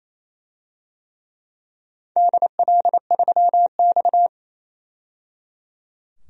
Morse code, heard as a keyed tone, about 28 wpm, 700 Hz. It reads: DL3X
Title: DXCC-100 Call Signs - 4 Characters - 28wpm